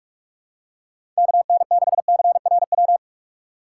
Morse code, heard as a keyed tone, about 46 wpm, 700 Hz. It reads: KN6CRW
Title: USA Call Signs - 6 Characters - 6x Character Spacing - 45wpm